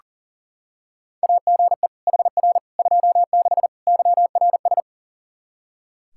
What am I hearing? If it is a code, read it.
Morse code, heard as a keyed tone, about 40 wpm, 700 Hz. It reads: AGE HR 26 YRS
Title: Practice QSO - Exchange 3 - More Info - 40wpm